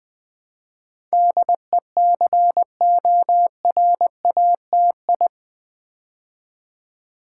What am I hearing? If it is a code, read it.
Morse code, heard as a keyed tone, about 20 wpm, 700 Hz. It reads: DECORATI
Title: Build A Word - 4 Letters At A Time - 10-Letter Words - 20wpm